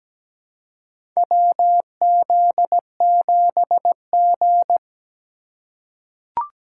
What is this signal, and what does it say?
Morse code, heard as a keyed tone, about 17 wpm, 700 Hz. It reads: WZ7G
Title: USA Call Signs - 3x Character Spacing - 17wpm